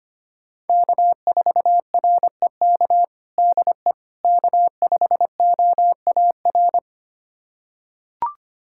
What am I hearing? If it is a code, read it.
Morse code, heard as a keyed tone, about 25 wpm, 700 Hz. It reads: K4REK DE K5OAR
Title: USA Callsign DE Callsign - 5 Characters - 25wpm